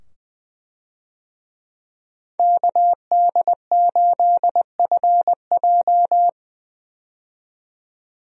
Morse code, heard as a keyed tone, about 21 wpm, 700 Hz. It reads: KD8FJ
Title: USA Call Signs - 5 Characters - 20wpm